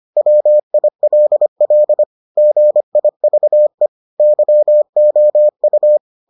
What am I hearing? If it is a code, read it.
Morse code, heard as a keyed tone, about 25 wpm, 600 Hz. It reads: WILL GIVE YOU